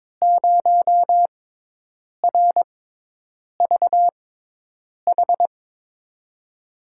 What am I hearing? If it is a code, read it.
Morse code, heard as a keyed tone, about 22 wpm, spaced as for 8 wpm, 700 Hz. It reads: 0RVH
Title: USA Call Signs - 5 Characters - 6x Character Spacing - 22wpm